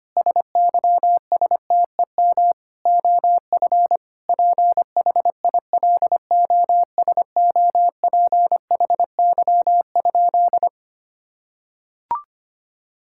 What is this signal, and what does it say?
Morse code, heard as a keyed tone, about 25 wpm, 700 Hz. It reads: SYSTEM OF PHILOSOPHY?
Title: Sentences from Top 1000 Words - Review Part 6 - 25wpm